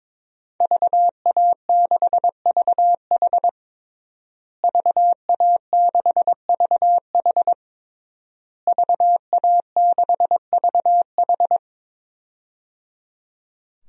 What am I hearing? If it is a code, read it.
Morse code, heard as a keyed tone, about 22 wpm, 700 Hz. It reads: VA6VH VA6VH VA6VH
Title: DXCC-100 Call Signs - 5 Characters - Repeated 3x - 22wpm